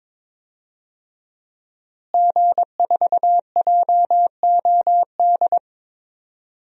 Morse code, heard as a keyed tone, about 22 wpm, 700 Hz. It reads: G4JOD
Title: DXCC-100 Call Signs - 5 Characters - Repeated 3x - 22wpm